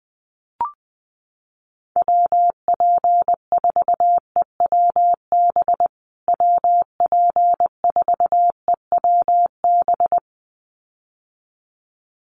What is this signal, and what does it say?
Morse code, heard as a keyed tone, about 20 wpm, 700 Hz. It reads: WP4EWB WP4EWB
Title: USA Call Signs Repeated 2x 20wpm